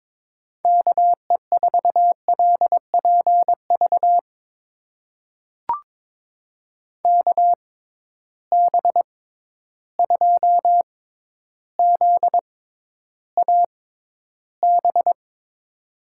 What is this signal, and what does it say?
Morse code, heard as a keyed tone, about 22 wpm, spaced as for 11 wpm, 700 Hz. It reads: KE4LPV KB2ZAB
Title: USA Call Signs - 6 Characters - 6x Character Spacing - 22wpm